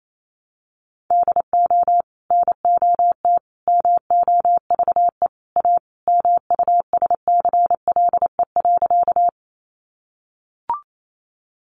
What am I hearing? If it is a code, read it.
Morse code, heard as a keyed tone, about 28 wpm, 700 Hz. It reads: DO NOT MOVE A MUSCLE.
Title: Sentences from Top 1300 Words 28wpm